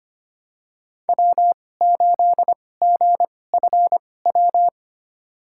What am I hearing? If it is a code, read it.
Morse code, heard as a keyed tone, about 25 wpm, spaced as for 18 wpm, 700 Hz. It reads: W8GFW
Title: USA Call Signs - 5 Characters - 2x Character Spacing - 25wpm